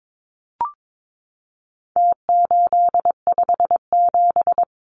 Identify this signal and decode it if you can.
Morse code, heard as a keyed tone, about 22 wpm, 700 Hz. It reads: T857
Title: Numbers - Contest Style - 4-Digit 22wpm